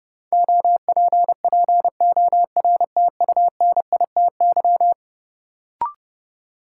Morse code, heard as a keyed tone, about 30 wpm, 700 Hz. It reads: OPPORTUNITY
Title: Sentences from Top 1000 Words - New Word Drill - 30wpm